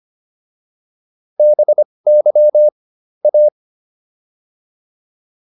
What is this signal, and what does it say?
Morse code, heard as a keyed tone, about 25 wpm, spaced as for 20 wpm, 600 Hz. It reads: BY A